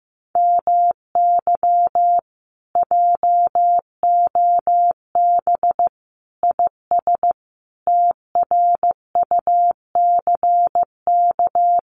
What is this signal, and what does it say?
Morse code, heard as a keyed tone, about 15 wpm, 700 Hz. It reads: MY JOB IS TRUCK
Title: Common QSO Phrases Encore 15wpm